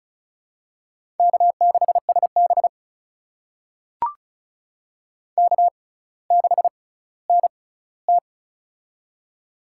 Morse code, heard as a keyed tone, about 36 wpm, spaced as for 18 wpm, 700 Hz. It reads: K6SB K6NT
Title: USA Call Signs - 6x Character Spacing - 35wpm